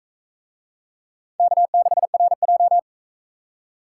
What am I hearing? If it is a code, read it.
Morse code, heard as a keyed tone, about 44 wpm, 700 Hz. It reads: K6RJ